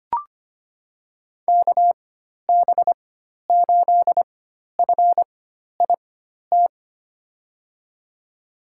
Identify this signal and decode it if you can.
Morse code, heard as a keyed tone, about 25 wpm, spaced as for 12 wpm, 700 Hz. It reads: KB8FIT
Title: USA Call Signs - 4x Character Spacing - 25wpm